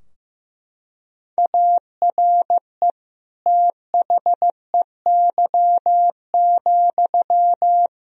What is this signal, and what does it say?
Morse code, heard as a keyed tone, about 15 wpm, 700 Hz. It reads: ARE THEY,
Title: Sentences from Top 1000 Words - Review Part 2 - 15wpm